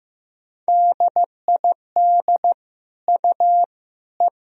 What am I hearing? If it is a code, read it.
Morse code, heard as a keyed tone, about 16 wpm, 700 Hz. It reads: DID U E
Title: Sentences from Top 400 Words and Abbreviations 15wpm